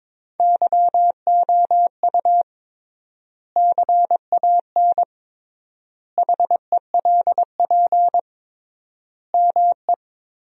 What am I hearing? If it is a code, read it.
Morse code, heard as a keyed tone, about 22 wpm, 700 Hz. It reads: YOU CAN HELP ME
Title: 3x Word-Spacing - Sentences from Top 200 Words - 22wpm